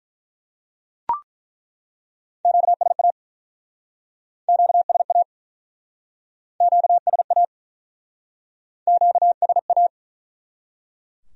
Morse code, heard as a keyed tone, about 48 wpm, 700 Hz. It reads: QSA QSA QSA QSA